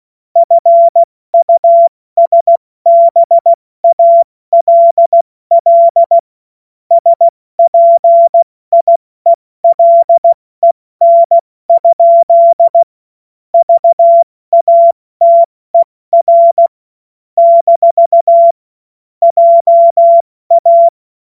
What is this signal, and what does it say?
Morse code, heard as a keyed tone, about 16 wpm, 680 Hz. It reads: FUSBALL SPIELEN? VATER - JA
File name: Morsekurs / Witze / Witz 07. September 2021, 70 F80